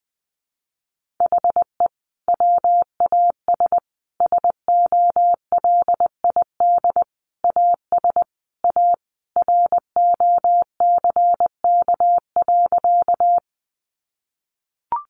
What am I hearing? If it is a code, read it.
Morse code, heard as a keyed tone, about 20 wpm, 700 Hz. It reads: HE WAS SOLID AS A ROCK.
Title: Sentences from Top 1500 Words 20wpm